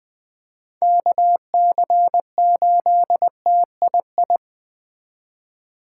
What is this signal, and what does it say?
Morse code, heard as a keyed tone, about 20 wpm, 700 Hz. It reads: KC8TII
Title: Speed Racing - USA 2x3 Call Signs 17wpm